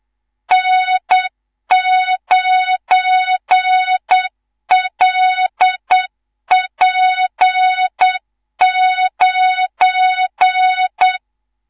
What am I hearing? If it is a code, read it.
Morse code, heard as a keyed tone, about 8 wpm, 3700 Hz, with noise. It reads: N9LP9
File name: Morze-HA2MN / hang / 19 GYAKORLAT